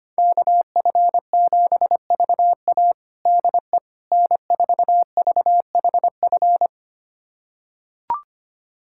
Morse code, heard as a keyed tone, about 25 wpm, 700 Hz. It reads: KF7VA DE N4VHF
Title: USA Callsign DE Callsign - 5 Characters - 25wpm